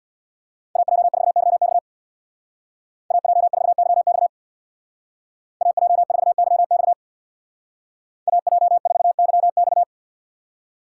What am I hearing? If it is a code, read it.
Morse code, heard as a keyed tone, about 58 wpm, 700 Hz. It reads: AJ4YX AJ4YX AJ4YX AJ4YX
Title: Speed Racing - USA 2x2 Call Signs 50wpm